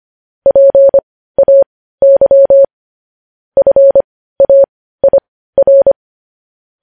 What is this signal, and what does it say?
Morse code, heard as a keyed tone, about 25 wpm, spaced as for 15 wpm, 550 Hz. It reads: PAY FAIR
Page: news headlines in morse code at 15 wpm